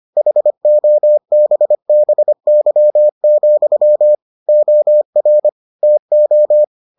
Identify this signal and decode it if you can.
Morse code, heard as a keyed tone, about 25 wpm, 600 Hz. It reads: HOBBY, OR TO